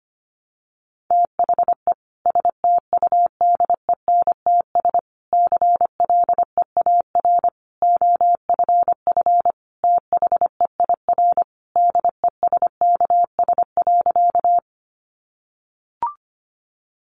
Morse code, heard as a keyed tone, about 25 wpm, 700 Hz. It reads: THE STUDENTS CLEAR OFF THEIR DESKS.
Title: Additional Easy Phrases 25wpm